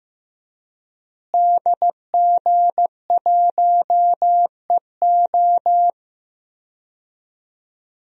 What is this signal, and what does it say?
Morse code, heard as a keyed tone, about 15 wpm, 700 Hz. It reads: DG1EO
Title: DXCC-100 Call Signs - 5 Characters - Repeated 3x - 15wpm